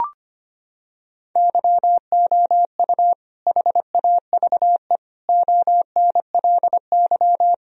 Morse code, heard as a keyed tone, about 25 wpm, 700 Hz. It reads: YOU HAVE ONLY
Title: Sentences from Top 1000 Words - Review Part 2 - 25wpm